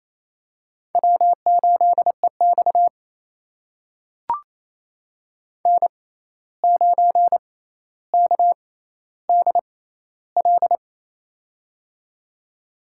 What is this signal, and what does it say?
Morse code, heard as a keyed tone, about 28 wpm, spaced as for 13 wpm, 700 Hz. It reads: W8EX N9KDL
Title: USA Call Signs - 6x Character Spacing - 28wpm